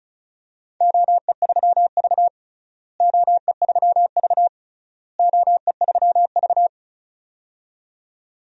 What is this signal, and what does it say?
Morse code, heard as a keyed tone, about 36 wpm, 700 Hz. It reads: OE3V OE3V OE3V
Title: DXCC-100 Call Signs - 4 Characters - Repeated 3x - 35wpm